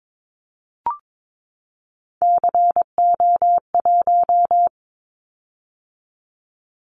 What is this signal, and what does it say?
Morse code, heard as a keyed tone, about 22 wpm, 700 Hz. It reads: CO1
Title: Callsign Prefixes for Top 100 DXCC - 22wpm